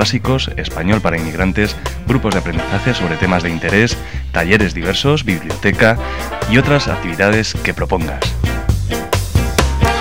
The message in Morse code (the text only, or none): none